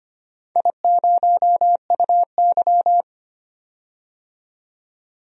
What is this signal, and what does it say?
Morse code, heard as a keyed tone, about 25 wpm, 700 Hz. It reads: I0UY
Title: DXCC-100 Call Signs - 4 Characters - 25wpm